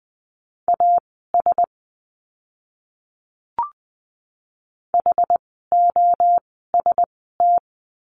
Morse code, heard as a keyed tone, about 20 wpm, spaced as for 14 wpm, 700 Hz. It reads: AS HOST
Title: 068 Lesson 12f - Letter d - Words 20wpm